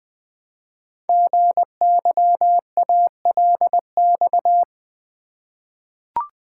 Morse code, heard as a keyed tone, about 20 wpm, 700 Hz. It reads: GYALX
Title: Code Groups - 5 Letters - 20wpm